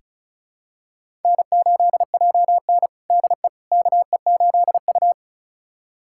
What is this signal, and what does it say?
Morse code, heard as a keyed tone, about 36 wpm, 700 Hz. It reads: N8JN DE KE8U
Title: USA Callsign DE Callsign - 4 Characters - 35wpm